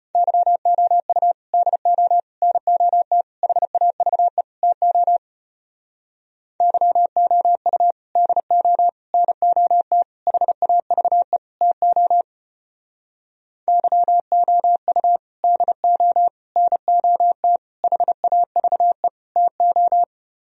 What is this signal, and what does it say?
Morse code, heard as a keyed tone, about 33 wpm, 700 Hz. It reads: YOU DO NOT HAVE TO YOU DO NOT HAVE TO YOU DO NOT HAVE TO